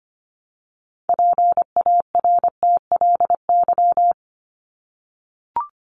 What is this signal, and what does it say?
Morse code, heard as a keyed tone, about 25 wpm, 700 Hz. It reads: PARTLY
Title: LY Suffix - Top 100 Words with Base Words 25wpm